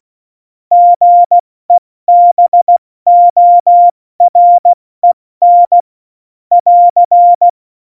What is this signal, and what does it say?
Morse code, heard as a keyed tone, about 16 wpm, 700 Hz. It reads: GEBOREN <AR>
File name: Morsekurs / Witze / Witz 31. Oktober 17, Tempo 70